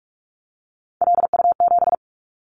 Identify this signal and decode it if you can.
Morse code, heard as a keyed tone, about 46 wpm, 700 Hz. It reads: LU7
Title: Callsign Prefixes for Top 100 DXCC - 45wpm